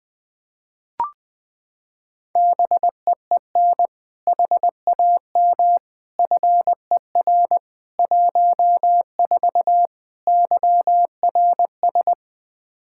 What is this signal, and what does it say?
Morse code, heard as a keyed tone, about 21 wpm, 700 Hz. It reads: BEEN HAM FER 14 YRS
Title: Practice QSO - Exchange 3 - More Info - 20wpm